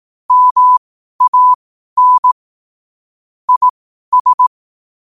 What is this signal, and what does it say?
Morse code, heard as a keyed tone, about 18 wpm, spaced as for 12 wpm, 1000 Hz. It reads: MAN IS
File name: qotdfiles / 12wpm